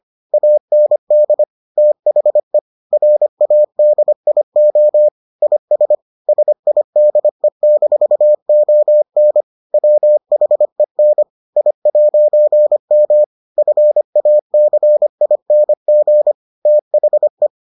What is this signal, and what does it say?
Morse code, heard as a keyed tone, about 25 wpm, 600 Hz. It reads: AND THE RADIO IS SIDE-ON WHEN I'M FACING THE